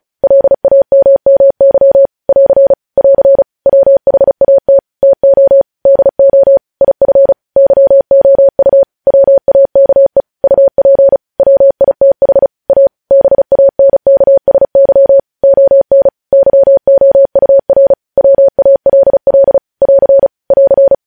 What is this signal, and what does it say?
Morse code, heard as a keyed tone, about 36 wpm, 550 Hz. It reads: LAMMY <AR> <AR> WHAT TO DO IF YOU WAKE UP WITH A BANKSY ON YOUR WALL <AR> <AR>
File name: podcast / News headlines-1x-35x35